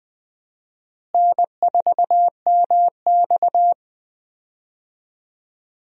Morse code, heard as a keyed tone, about 20 wpm, 700 Hz. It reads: N4MX
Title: USA Call Signs - 4 Characters - Repeated 3x - 20wpm